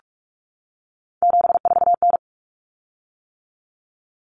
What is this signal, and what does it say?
Morse code, heard as a keyed tone, about 44 wpm, 700 Hz. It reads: 74N
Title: Common Cut Numbers - 3-Digit 45wpm